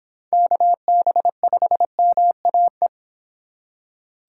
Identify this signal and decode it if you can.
Morse code, heard as a keyed tone, about 25 wpm, 700 Hz. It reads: KB5MAE